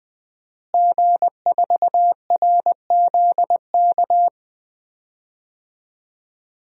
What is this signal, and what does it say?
Morse code, heard as a keyed tone, about 21 wpm, 700 Hz. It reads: G4RZK
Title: DXCC-100 Call Signs - 5 Characters - Repeated 3x - 20wpm